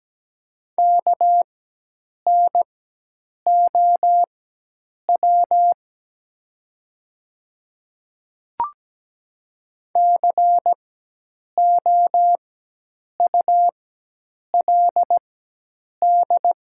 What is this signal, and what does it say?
Morse code, heard as a keyed tone, about 17 wpm, spaced as for 8 wpm, 700 Hz. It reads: KNOW COULD
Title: 4x Character Spacing - Top 100 Words - 17wpm